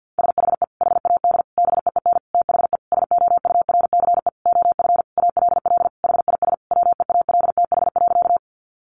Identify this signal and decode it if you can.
Morse code, heard as a keyed tone, about 50 wpm, 700 Hz. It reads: SHE HAD BEEN THE SOURCE OF ALL HIS WEALTH.
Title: Sentences from Top 1200 Words 50wpm